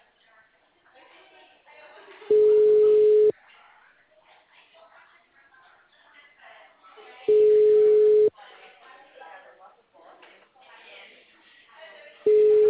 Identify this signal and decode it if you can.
Ringback tone (Europe, 425 Hz), with speech.
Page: Leme 1